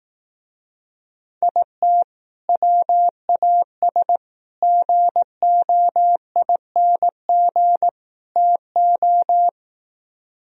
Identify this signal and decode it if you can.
Morse code, heard as a keyed tone, about 18 wpm, 700 Hz. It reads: IT WAS GOING TO
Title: Speed-Racing - Sets of 4 Words 15wpm